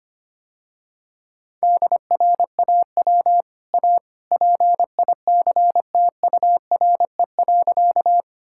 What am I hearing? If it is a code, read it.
Morse code, heard as a keyed tone, about 25 wpm, 700 Hz. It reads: DRAW A PICTURE.
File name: Easy Sentences with Hints - 25wpm